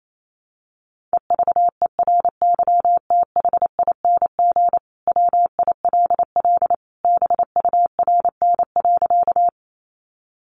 Morse code, heard as a keyed tone, about 28 wpm, 700 Hz. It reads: EVERYTHING WILL BURN.
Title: Sentences from Top 1300 Words 28wpm